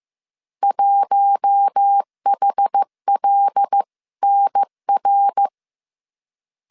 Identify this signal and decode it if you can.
Morse code, heard as a keyed tone, about 15 wpm, 800 Hz. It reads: 1HLNR
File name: CW / DET / 1 6 1500C S B 14wpm 2958